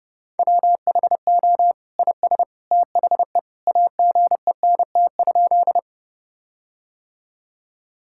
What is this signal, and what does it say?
Morse code, heard as a keyed tone, about 32 wpm, 700 Hz. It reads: WHO IS THE AGENT?
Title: Sentences from Top 1600 Words 30wpm